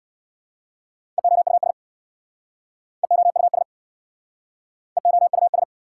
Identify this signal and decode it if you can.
Morse code, heard as a keyed tone, about 69 wpm, 700 Hz. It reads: EOUS EOUS EOUS